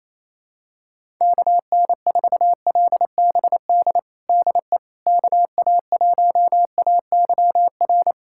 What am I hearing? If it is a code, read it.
Morse code, heard as a keyed tone, about 28 wpm, 700 Hz. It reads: KN4LBD DE KA1AYR
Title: USA Callsign DE Callsign - 6 Characters - 28wpm